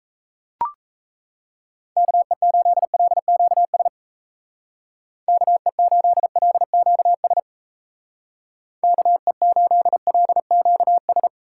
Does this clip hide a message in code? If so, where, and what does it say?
Yes: Morse code from 2.0 to 11.3 s, heard as a keyed tone, about 38 wpm, 700 Hz. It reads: KE8LQS KE8LQS KE8LQS